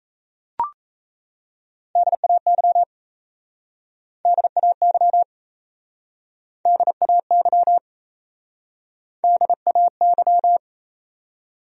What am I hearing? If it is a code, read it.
Morse code, heard as a keyed tone, about 37 wpm, 700 Hz. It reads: DAY DAY DAY DAY